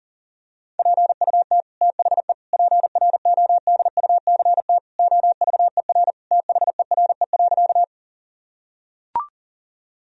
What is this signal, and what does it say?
Morse code, heard as a keyed tone, about 40 wpm, 700 Hz. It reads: PUT THE PRODUCT OVER THERE.
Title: Sentences from Top 1000 Words - Review Part 2 - 40wpm